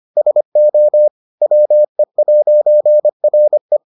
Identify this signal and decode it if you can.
Morse code, heard as a keyed tone, about 25 wpm, 600 Hz. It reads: SO WE'RE